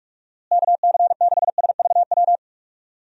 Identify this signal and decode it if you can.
Morse code, heard as a keyed tone, about 43 wpm, 700 Hz. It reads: KC6SVW
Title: USA Call Signs - 6 Characters - Repeated 2x - 45wpm